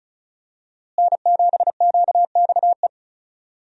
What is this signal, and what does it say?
Morse code, heard as a keyed tone, about 34 wpm, 700 Hz. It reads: N7QXE
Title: USA Call Signs - 5 Characters - Repeated 2x - 35wpm